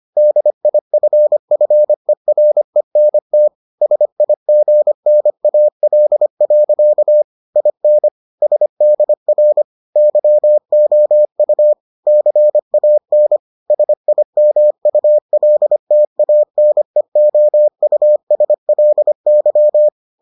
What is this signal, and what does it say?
Morse code, heard as a keyed tone, about 25 wpm, 600 Hz. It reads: DIFFERENT SIGNAL. IN SDR YOU CAN SIMULTANEOUSLY